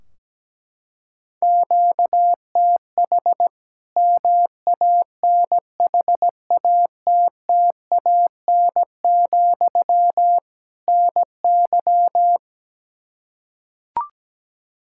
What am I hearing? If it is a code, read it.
Morse code, heard as a keyed tone, about 17 wpm, 700 Hz. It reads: QTH MANHATTAN, NY
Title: Practice QSO - Exchange 1 - Beginning - 17wpm